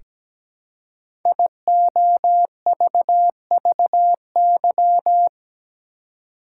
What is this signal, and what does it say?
Morse code, heard as a keyed tone, about 17 wpm, 700 Hz. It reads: IOVVY
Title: Code Groups - 5 Letter:Numbers - 17wpm